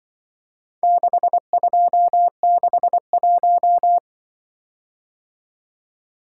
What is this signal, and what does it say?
Morse code, heard as a keyed tone, about 22 wpm, 700 Hz. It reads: 6261
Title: Speed-Racing - Four Letter-Number Combinations - 20wpm